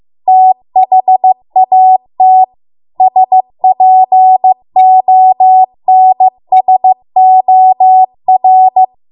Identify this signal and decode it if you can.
Morse code, heard as a keyed tone, about 15 wpm, 750 Hz. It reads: THAT SPONSOR